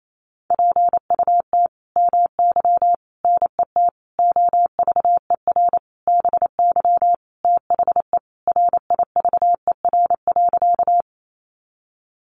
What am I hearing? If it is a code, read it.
Morse code, heard as a keyed tone, about 28 wpm, 700 Hz. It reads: PUT MY NET OVER BY THE RIVER.
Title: Sentences from Top 1400 Words 28wpm